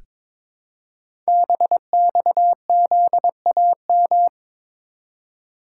Morse code, heard as a keyed tone, about 22 wpm, 700 Hz. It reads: BXZAM
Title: Code Groups - 5 Letters - 22wpm